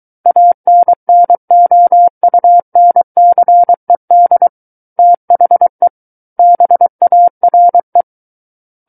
Morse code, heard as a keyed tone, about 23 wpm, 700 Hz. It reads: ANNOUNCED THE BARE
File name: Book One / 14-Chapter 14 - 23 WPM